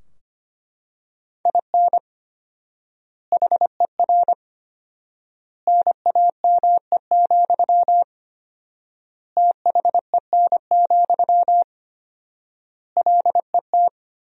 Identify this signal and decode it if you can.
Morse code, heard as a keyed tone, about 25 wpm, 700 Hz. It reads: IN HER NAME, THEN, LET
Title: 4x Word-Spacing - Sentences from Top 500 Words - 25wpm